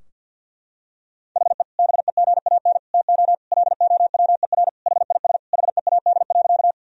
Morse code, heard as a keyed tone, about 50 wpm, 700 Hz. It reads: HE BEGAN TO LOWER HIS HEAD.
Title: Sentences from Top 1000 Words - Review Part 4 - 50wpm